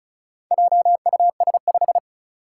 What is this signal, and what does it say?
Morse code, heard as a keyed tone, about 36 wpm, 700 Hz. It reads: JUS5
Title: Four Challenging Letter-Number Combinations 35wpm